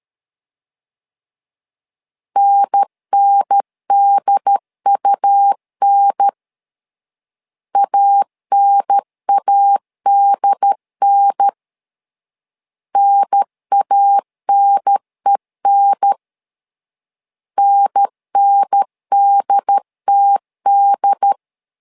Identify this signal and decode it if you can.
Morse code, heard as a keyed tone, about 13 wpm, 800 Hz. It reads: NNDUN ANADN NANEN NNDTD
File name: LEZIONI / 2 UHDN / 2 UHDN 12wpm 1456